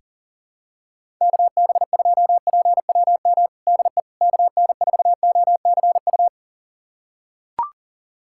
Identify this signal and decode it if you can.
Morse code, heard as a keyed tone, about 40 wpm, 700 Hz. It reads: KB2PWM DE KN4OCU
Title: USA Callsign DE Callsign - 6 Characters - 40wpm